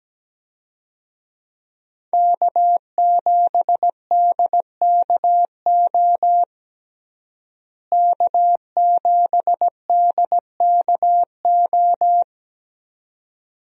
Morse code, heard as a keyed tone, about 17 wpm, 700 Hz. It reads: K7DKO K7DKO
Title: USA Call Signs - 5 Characters - Repeated 2x - 17wpm